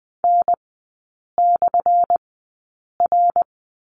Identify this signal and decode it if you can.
Morse code, heard as a keyed tone, about 20 wpm, spaced as for 8 wpm, 700 Hz. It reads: N/R
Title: 178 Lesson 29e - Slash - Three Characters 20wpm